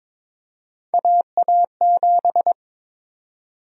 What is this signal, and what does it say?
Morse code, heard as a keyed tone, about 22 wpm, 700 Hz. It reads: AA7
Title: USA Call Sign Prefixes - Repeated 2x - 22wpm